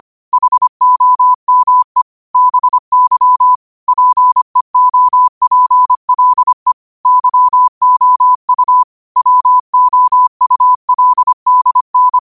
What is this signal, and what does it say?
Morse code, heard as a keyed tone, about 25 wpm, 1000 Hz. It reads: HOME BY PEOPLE YOU WOULDN
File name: qotdfiles / 25wpm